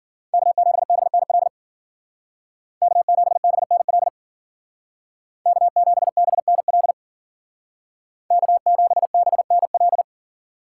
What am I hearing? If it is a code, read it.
Morse code, heard as a keyed tone, about 53 wpm, 700 Hz. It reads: K7BNL K7BNL K7BNL K7BNL